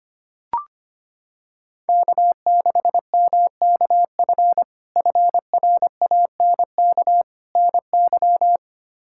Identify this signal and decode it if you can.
Morse code, heard as a keyed tone, about 25 wpm, 700 Hz. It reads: K6MKF FRANK NY
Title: K1USN SST QSO Practice - 25wpm